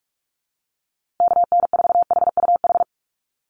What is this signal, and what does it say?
Morse code, heard as a keyed tone, about 44 wpm, 700 Hz. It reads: KN4HUH
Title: USA Call Signs Repeated 3x 45wpm